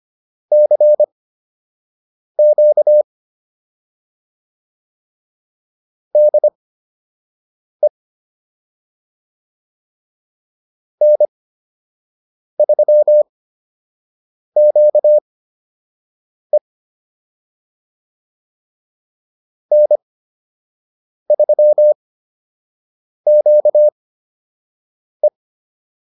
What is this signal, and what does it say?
Morse code, heard as a keyed tone, about 25 wpm, spaced as for 6 wpm, 600 Hz. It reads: CQ DE N3QE N3QE